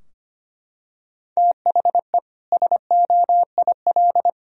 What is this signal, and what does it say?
Morse code, heard as a keyed tone, about 25 wpm, 700 Hz. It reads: THE SOIL
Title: Sentences from Top 1000 Words - Review Part 4 - 25wpm